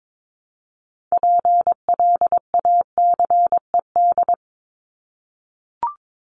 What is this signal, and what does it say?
Morse code, heard as a keyed tone, about 22 wpm, 700 Hz. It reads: PLACED
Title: ED Suffix - Top 100 Words with Base Words 22wpm